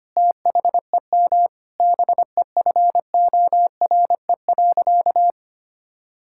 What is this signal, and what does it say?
Morse code, heard as a keyed tone, about 25 wpm, 700 Hz. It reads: THEM BEFORE.
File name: practice / 25wpm / Sentences with 9-Word Phrases - 25wpm